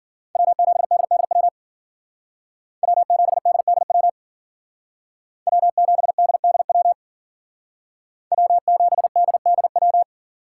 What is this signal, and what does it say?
Morse code, heard as a keyed tone, about 53 wpm, 700 Hz. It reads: W7DDW W7DDW W7DDW W7DDW